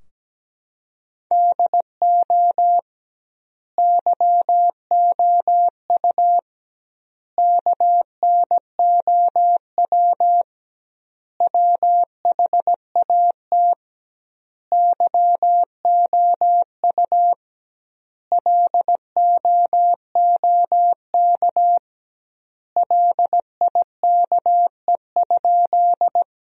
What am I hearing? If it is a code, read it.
Morse code, heard as a keyed tone, about 17 wpm, 700 Hz. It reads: DO YOU KNOW WHAT YOU LOOK LIKE?
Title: 2x Word-Spacing - Sentences from Top 100 Words - 17wpm